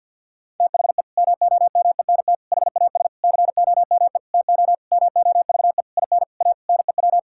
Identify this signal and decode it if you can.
Morse code, heard as a keyed tone, about 50 wpm, 700 Hz. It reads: THE MOMENT HAS COME TO MOVE IN A NEW
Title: Sentences from Top 1000 Words - Review Part 5 - 50wpm